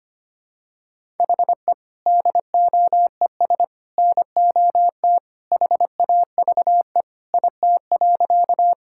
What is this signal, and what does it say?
Morse code, heard as a keyed tone, about 25 wpm, 700 Hz. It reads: HE DOES NOT HAVE IT.
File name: practice / 25wpm / Sentences with 3-Word Phrases - 25wpm